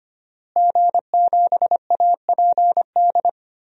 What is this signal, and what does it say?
Morse code, heard as a keyed tone, about 25 wpm, 700 Hz. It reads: G7APD
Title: DXCC-100 Call Signs - 5 Characters - Repeated 3x - 25wpm